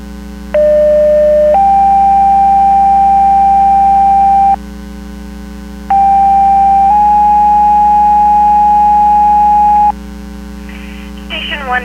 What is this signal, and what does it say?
Two-tone page, heard as a one-second tone, then a three-second tone, about 600 then 790 Hz, about 790 then 830 Hz, with speech. It is heard from 0.5 to 9.9 s.